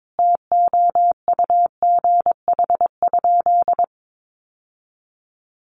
Morse code, heard as a keyed tone, about 22 wpm, 700 Hz. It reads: TOUGH?
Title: Sentences from Top 1200 Words 22wpm